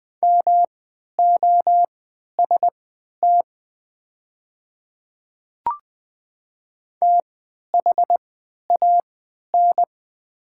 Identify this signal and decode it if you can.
Morse code, heard as a keyed tone, about 20 wpm, spaced as for 11 wpm, 700 Hz. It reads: MOST THAN
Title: 3x Character Spacing - Top 100 Words - 20wpm